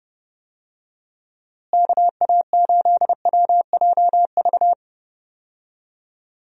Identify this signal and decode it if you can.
Morse code, heard as a keyed tone, about 30 wpm, 700 Hz. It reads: KA8WJV
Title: USA Call Signs - 6 Characters - Repeated 2x - 30wpm